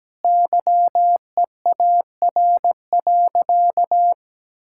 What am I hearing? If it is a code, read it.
Morse code, heard as a keyed tone, about 17 wpm, 700 Hz. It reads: YEAR.
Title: Sentences from Top 1600 Words 17wpm